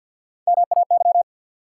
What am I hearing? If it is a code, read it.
Morse code, heard as a keyed tone, about 53 wpm, 700 Hz. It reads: MAY
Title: Sets of 2 Words - New Word Drill - 50wpm